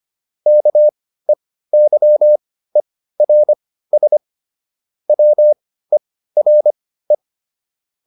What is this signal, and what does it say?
Morse code, heard as a keyed tone, about 25 wpm, spaced as for 15 wpm, 600 Hz. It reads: KEYERS WERE
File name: SS215 15